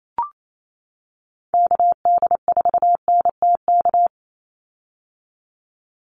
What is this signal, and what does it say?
Morse code, heard as a keyed tone, about 28 wpm, 700 Hz. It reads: KD4NTK
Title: USA Call Signs 28wpm